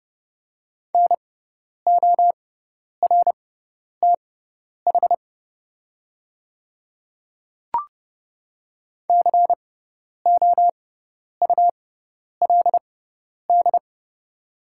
Morse code, heard as a keyed tone, about 30 wpm, spaced as for 10 wpm, 700 Hz. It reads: NORTH COULD
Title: 6x Character Spacing - Top 300 Words - 30wpm